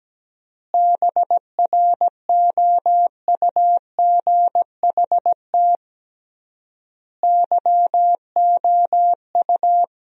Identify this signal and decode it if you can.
Morse code, heard as a keyed tone, about 17 wpm, 700 Hz. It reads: BROUGHT YOU
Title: 3x Word-Spacing - Sentences from Top 500 Words - 17wpm